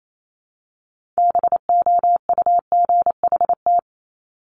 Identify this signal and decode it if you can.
Morse code, heard as a keyed tone, about 28 wpm, 700 Hz. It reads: BOUGHT
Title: Multi-letter Phonemes 28wpm